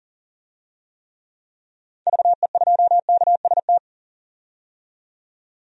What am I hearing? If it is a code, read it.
Morse code, heard as a keyed tone, about 40 wpm, 700 Hz. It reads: VE2KST
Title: DXCC-100 Call Signs - 6 Characters - 40wpm